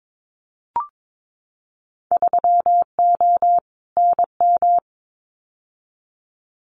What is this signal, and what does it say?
Morse code, heard as a keyed tone, about 22 wpm, 700 Hz. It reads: 3O NM
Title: Winter Field Day Exchanges Encore 22wpm